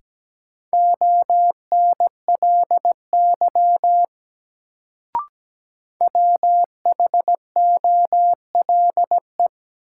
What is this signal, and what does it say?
Morse code, heard as a keyed tone, about 17 wpm, 700 Hz. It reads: ONLY WHOLE
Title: Sets of 2 Words - New Word Drill - 17wpm